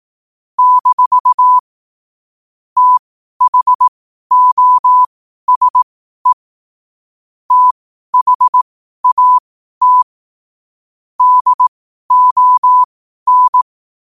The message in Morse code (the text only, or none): - THOSE THAT DON